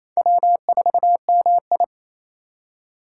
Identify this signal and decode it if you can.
Morse code, heard as a keyed tone, about 28 wpm, 700 Hz. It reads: W4MI
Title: USA Call Signs - 4 Characters - 3x Character Spacing - 28wpm